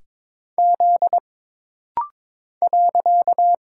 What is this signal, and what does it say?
Morse code, heard as a keyed tone, about 22 wpm, 700 Hz. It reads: Z .
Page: This rapid-fire timing is geared to push your ICR (Instant Character Recognition) skills.